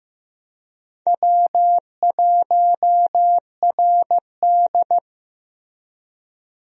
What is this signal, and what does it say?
Morse code, heard as a keyed tone, about 16 wpm, 700 Hz. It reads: W1RD
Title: USA Call Signs - 4 Characters - Repeated 2x - 15wpm